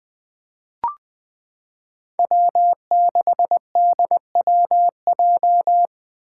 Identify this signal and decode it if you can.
Morse code, heard as a keyed tone, about 20 wpm, 700 Hz. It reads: W6DWJ
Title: Five Challenging Letter-Number Combinations 20wpm